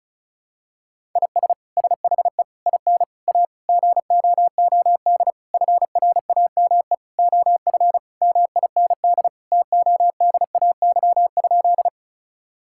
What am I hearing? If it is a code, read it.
Morse code, heard as a keyed tone, about 36 wpm, 700 Hz. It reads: IS SHE IN A GOOD FRAME OF MIND TODAY?
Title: Sentences from Top 1000 Words - Review Part 6 - 35wpm